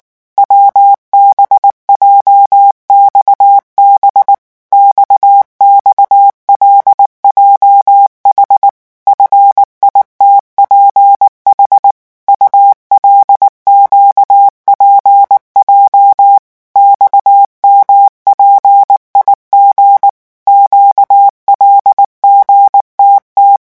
Morse code, heard as a keyed tone, about 19 wpm, 800 Hz. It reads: WBJXB XXLJH FITPH ULQPJ XMPIG QLGTT
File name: morse mp3 / revision7-vitesse 18 mots